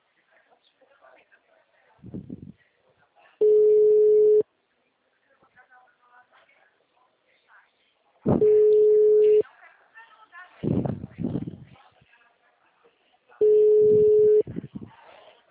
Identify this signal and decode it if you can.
Ringback tone (Europe, 425 Hz), with speech.
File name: out- -2013- - - .22